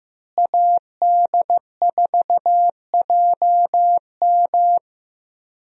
Morse code, heard as a keyed tone, about 15 wpm, 700 Hz. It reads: AD4JM
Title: USA Call Signs - 5 Characters - Repeated 3x - 15wpm